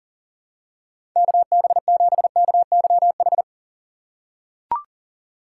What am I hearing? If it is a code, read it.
Morse code, heard as a keyed tone, about 40 wpm, 700 Hz. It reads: KB7KYH
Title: USA Call Signs - 6 Characters - 2x Character Spacing - 40wpm